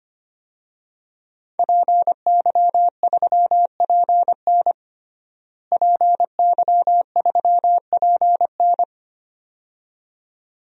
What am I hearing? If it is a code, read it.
Morse code, heard as a keyed tone, about 25 wpm, 700 Hz. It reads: PY3PN PY3PN
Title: DXCC-100 Call Signs - 5 Characters - Repeated 2x - 25wpm